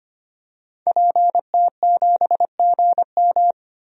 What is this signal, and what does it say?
Morse code, heard as a keyed tone, about 25 wpm, 700 Hz. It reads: PT7GM
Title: DXCC-100 Call Signs - 5 Characters - 25wpm